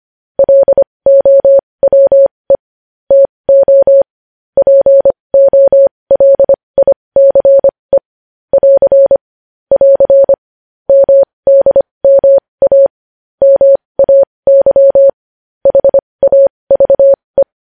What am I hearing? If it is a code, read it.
Morse code, heard as a keyed tone, about 25 wpm, spaced as for 20 wpm, 550 Hz. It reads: LOWE TO POLICE <AR> <AR> MDMA MAY HAVE
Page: news-headlines-in-morse-code-at-20-wpm